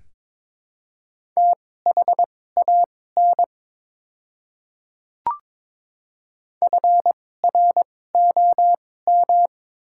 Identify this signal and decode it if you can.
Morse code, heard as a keyed tone, about 22 wpm, spaced as for 16 wpm, 700 Hz. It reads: THAN FROM
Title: 2x Character Spacing - Top 100 Words - 22wpm